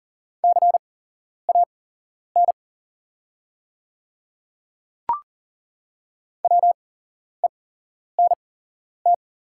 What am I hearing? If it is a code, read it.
Morse code, heard as a keyed tone, about 40 wpm, spaced as for 11 wpm, 700 Hz. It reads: CAN WENT
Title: 8x Character Spacing - Top 300 Words - 40wpm